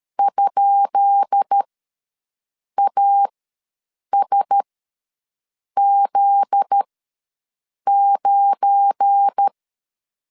Morse code, heard as a keyed tone, about 13 wpm, spaced as for 6 wpm, 800 Hz. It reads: ?ASZ9
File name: LEZIONI / 10 SIGNS / 10D COMA ws 12wpm 3618